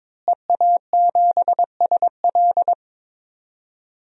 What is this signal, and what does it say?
Morse code, heard as a keyed tone, about 22 wpm, 700 Hz. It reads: EA7SL
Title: DXCC-100 Call Signs - 5 Characters - Repeated 3x - 22wpm